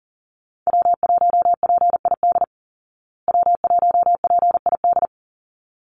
Morse code, heard as a keyed tone, about 40 wpm, 700 Hz. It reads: W1PID W1PID
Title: Call Signs Repeated 3x - 40wpm